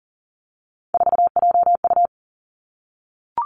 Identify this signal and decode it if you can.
Morse code, heard as a keyed tone, about 38 wpm, 700 Hz. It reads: 4JU
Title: Three Challenging Letter-Number Combinations 40wpm